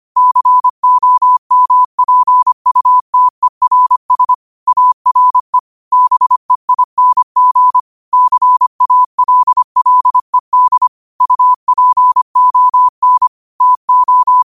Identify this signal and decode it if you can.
Morse code, heard as a keyed tone, about 25 wpm, 1000 Hz. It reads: COMPUTERS ARE BEING CALLED UPON TO
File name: 25wpm